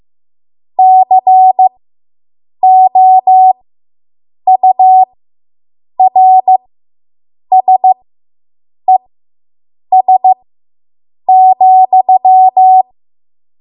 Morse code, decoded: COURSES,

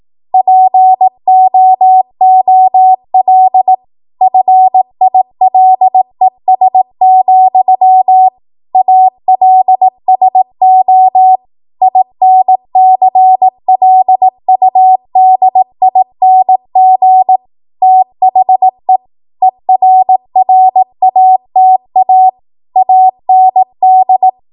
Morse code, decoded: POOL FILES, ALSO INCLUDING THE ERRATA AND

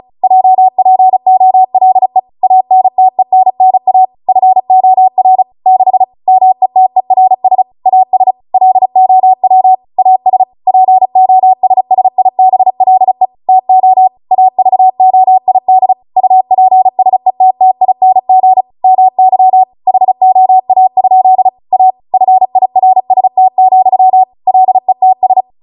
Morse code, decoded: JPOLE ANTENNA FOR 6 METERS AS LOW AS POSSIBLE TO AVOID UPSETTING MY HOA? A FIRST, LETS

